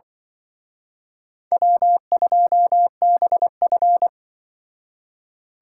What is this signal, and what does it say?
Morse code, heard as a keyed tone, about 25 wpm, 700 Hz. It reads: W2BF
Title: Speed Racing - USA 1x2 Call Signs 20wpm